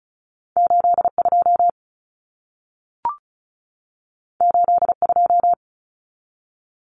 Morse code, heard as a keyed tone, about 36 wpm, 700 Hz. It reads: 82 82
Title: Common Cut Numbers - 2-Digit 35wpm